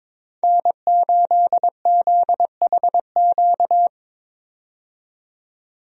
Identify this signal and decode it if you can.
Morse code, heard as a keyed tone, about 22 wpm, 700 Hz. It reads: N8ZHQ
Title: USA Call Signs - 5 Characters - 22wpm